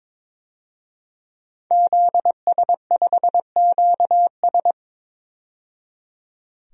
Morse code, heard as a keyed tone, about 22 wpm, 700 Hz. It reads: ZS5QS
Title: DXCC-100 Call Signs - 5 Characters - 22wpm